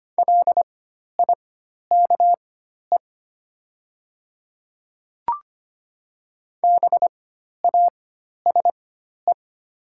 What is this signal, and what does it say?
Morse code, heard as a keyed tone, about 25 wpm, spaced as for 12 wpm, 700 Hz. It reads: LIKE BASE
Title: 4x Character Spacing - Top 300 Words - 25wpm